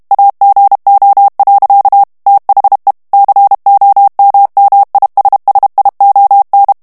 Morse code, heard as a keyed tone, about 32 wpm, 800 Hz. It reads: AGO. THE COMMISSION
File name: ANGLAIS texte clair 3 vitesse 30 mots minute